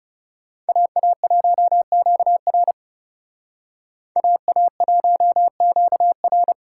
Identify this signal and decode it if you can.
Morse code, heard as a keyed tone, about 38 wpm, 700 Hz. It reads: AA1QR AA1QR